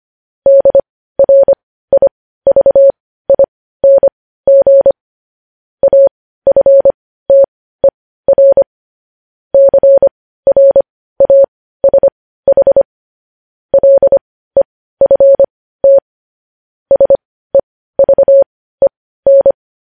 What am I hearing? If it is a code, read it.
Morse code, heard as a keyed tone, about 25 wpm, spaced as for 15 wpm, 550 Hz. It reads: DRIVING AFTER CRASH LEFT SEVEN
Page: news-headlines-in-morse-code-at-15-wpm